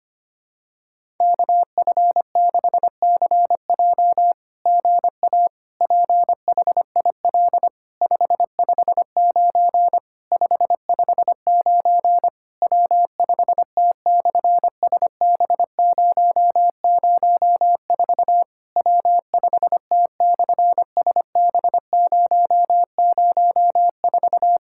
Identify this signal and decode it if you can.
Morse code, heard as a keyed tone, about 24 wpm, 700 Hz. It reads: KF6CJ GA PHIL 559 559 W5T/SB004 W5T/SB004
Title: SOTA QSO Practice 25wpm